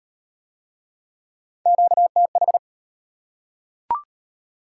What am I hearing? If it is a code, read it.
Morse code, heard as a keyed tone, about 31 wpm, 690 Hz. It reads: QTH
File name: practice / 25wpm / Speed-Racing Top 20 QSO Elements - 25wpm